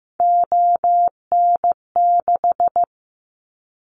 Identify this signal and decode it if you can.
Morse code, heard as a keyed tone, about 15 wpm, 700 Hz. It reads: ON6
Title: Callsign Prefixes for Top 100 DXCC - 15wpm